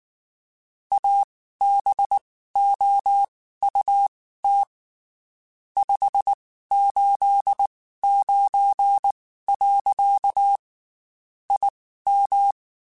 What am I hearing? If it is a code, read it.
Morse code, heard as a keyed tone, about 19 wpm, spaced as for 14 wpm, 800 Hz. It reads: ABOUT 589. IM
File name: QSOs / QSOs-mp3s / WD0FBT 1@18